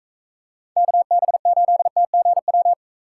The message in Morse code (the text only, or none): KB8TGW